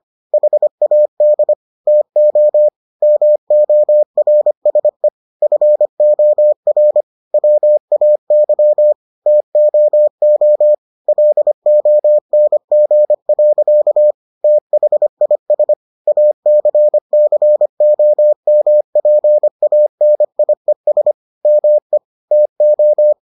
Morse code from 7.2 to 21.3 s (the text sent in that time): WAY TOO LONG. THIS ACCOMPANIES